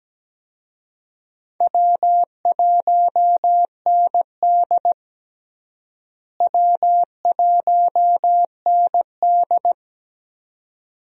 Morse code, heard as a keyed tone, about 17 wpm, 700 Hz. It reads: W1ND W1ND
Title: USA Call Signs - 4 Characters - Repeated 2x - 17wpm